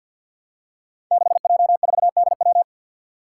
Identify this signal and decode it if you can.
Morse code, heard as a keyed tone, about 52 wpm, 700 Hz. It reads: 6J4DW
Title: Five Challenging Letter-Number Combinations 50wpm